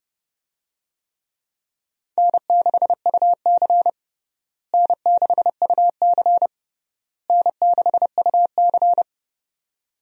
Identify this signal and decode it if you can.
Morse code, heard as a keyed tone, about 30 wpm, 700 Hz. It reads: N6UC N6UC N6UC
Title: USA Call Signs - 4 Characters - Repeated 3x - 30wpm